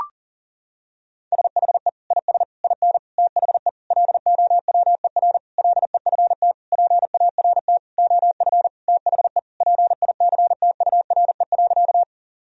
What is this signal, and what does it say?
Morse code, heard as a keyed tone, about 40 wpm, 700 Hz. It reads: SHE IS IN THE LOWER LEFT PART OF THE PICTURE.
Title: Sentences from Top 1000 Words - Review Part 4 - 40wpm